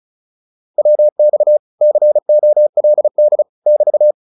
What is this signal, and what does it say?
Morse code, heard as a keyed tone, about 34 wpm, 600 Hz. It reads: WX COLD <BT>